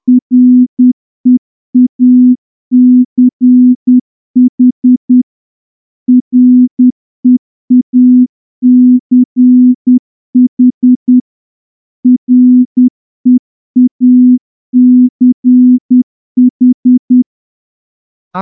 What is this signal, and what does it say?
Morse code, heard as a keyed tone, about 10 wpm, 250 Hz. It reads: REACH REACH REACH
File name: morse-training-10WPM